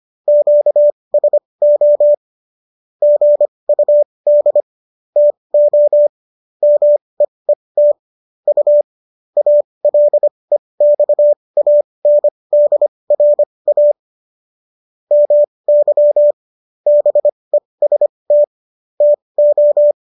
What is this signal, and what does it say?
Morse code, heard as a keyed tone, about 25 wpm, spaced as for 20 wpm, 590 Hz. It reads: QSO GUD TO MEET U ALEXANDRA MY BEST TO